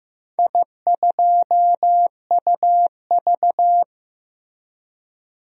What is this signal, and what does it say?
Morse code, heard as a keyed tone, about 15 wpm, 700 Hz. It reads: I2UV
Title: DXCC-100 Call Signs - 4 Characters - 15wpm